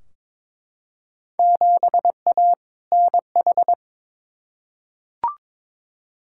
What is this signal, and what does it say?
Morse code, heard as a keyed tone, about 22 wpm, 700 Hz. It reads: 7A NH
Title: Field Day Exchanges - 22wpm